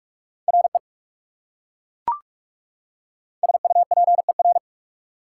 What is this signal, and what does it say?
Morse code, heard as a keyed tone, about 38 wpm, 700 Hz. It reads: RE SUPER